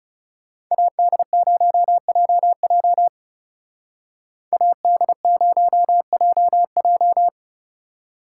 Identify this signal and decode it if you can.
Morse code, heard as a keyed tone, about 36 wpm, 700 Hz. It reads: AD0JJ AD0JJ
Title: Speed Racing - USA Call Signs 30wpm